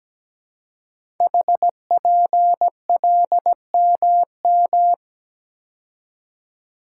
Morse code, heard as a keyed tone, about 17 wpm, 700 Hz. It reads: HPLMM
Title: Code Groups - 5 Letters - 17wpm